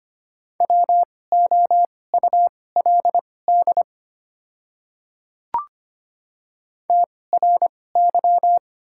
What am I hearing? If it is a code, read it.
Morse code, heard as a keyed tone, about 25 wpm, spaced as for 18 wpm, 700 Hz. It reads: WOULD TRY
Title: 2x Character Spacing - Top 500 Words - 25wpm